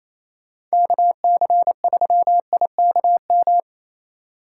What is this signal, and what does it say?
Morse code, heard as a keyed tone, about 33 wpm, 700 Hz. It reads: KC3IKM
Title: Speed Racing - USA Call Signs 28wpm